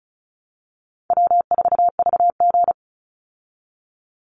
Morse code, heard as a keyed tone, about 34 wpm, 700 Hz. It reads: W4VG
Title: Easy Call Signs Repeated 2x - 35wpm